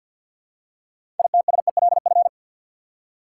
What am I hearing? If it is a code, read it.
Morse code, heard as a keyed tone, about 53 wpm, 700 Hz. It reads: ITSELF
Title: Six Letter Words - 50wpm